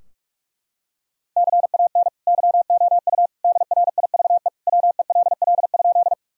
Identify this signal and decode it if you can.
Morse code, heard as a keyed tone, about 45 wpm, 700 Hz. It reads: CAN YOU DRIVE WELL?
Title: Sentences from Top 1000 Words - Review Part 2 - 45wpm